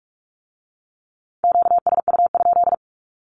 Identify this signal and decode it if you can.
Morse code, heard as a keyed tone, about 46 wpm, 700 Hz. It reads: QSU?
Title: Q-Codes with Spoken Meaning - 45wpm